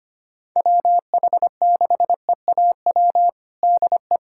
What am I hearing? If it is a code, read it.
Morse code, heard as a keyed tone, about 25 wpm, 700 Hz. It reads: WH6EAW DE
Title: USA Callsign DE Callsign 25wpm